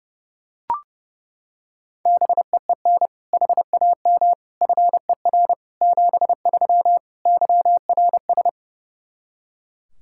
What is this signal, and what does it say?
Morse code, heard as a keyed tone, about 32 wpm, 700 Hz. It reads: BEEN HAM FER 73 YRS
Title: Practice QSO - Exchange 3 - More Info - 30wpm